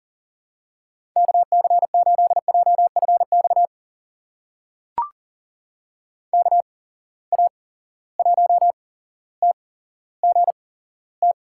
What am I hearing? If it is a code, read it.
Morse code, heard as a keyed tone, about 40 wpm, spaced as for 17 wpm, 700 Hz. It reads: KC8JFX KA1TGT
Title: USA Call Signs - 6 Characters - 8x Character Spacing - 40wpm